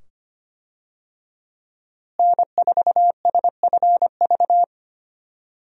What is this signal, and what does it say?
Morse code, heard as a keyed tone, about 25 wpm, 700 Hz. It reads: N4SFV
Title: USA Call Signs - 5 Characters - Repeated 2x - 25wpm